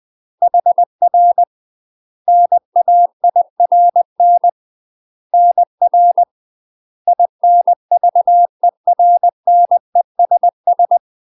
Morse code, heard as a keyed tone, about 20 wpm, 700 Hz. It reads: HR NAIRN NR INVERNESS
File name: UK MASTER QSOs TRAINING / 20WPM / UK MASTER QSOs 0058